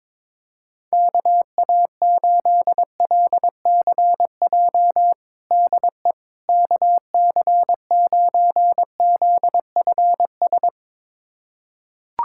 Morse code, heard as a keyed tone, about 22 wpm, 700 Hz. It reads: KA8LCJ DE KC9ZFS
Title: USA Callsign DE Callsign - 6 Characters - 22wpm